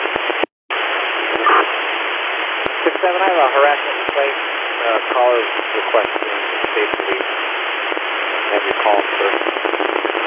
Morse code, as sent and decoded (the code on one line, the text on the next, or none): none